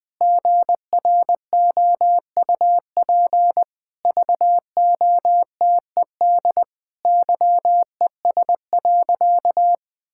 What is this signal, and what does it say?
Morse code, heard as a keyed tone, about 20 wpm, 700 Hz. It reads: GROUP VOTED YES.
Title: Sentences with 6-Word Phrases - 20wpm